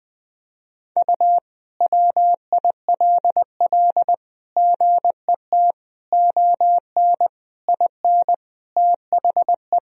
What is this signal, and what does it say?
Morse code, heard as a keyed tone, about 20 wpm, 700 Hz. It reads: U WILL GET ON IN THE
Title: Sentences from Top 200 Words and Abbreviations 20wpm